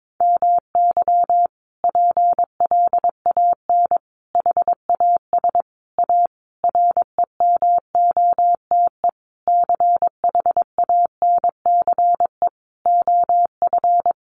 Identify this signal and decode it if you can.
Morse code, heard as a keyed tone, about 22 wpm, 700 Hz. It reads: MY PLAN HAS A REMOTE CHANCE OF
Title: Sentences from Top 1300 Words 22wpm